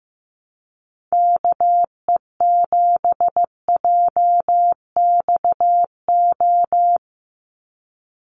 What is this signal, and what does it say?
Morse code, heard as a keyed tone, about 15 wpm, 700 Hz. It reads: KE7JXO
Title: USA Call Signs 15wpm